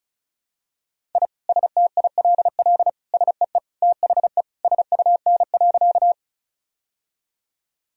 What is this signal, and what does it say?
Morse code, heard as a keyed tone, about 36 wpm, 700 Hz. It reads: I STILL SEE THE SUN.
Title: Sentences from Top 1000 Words - Review Part 1 - 35wpm